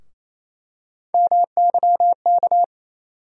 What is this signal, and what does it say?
Morse code, heard as a keyed tone, about 28 wpm, 700 Hz. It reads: MYK
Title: USA Call Sign Postfixes - 3 Letters 28wpm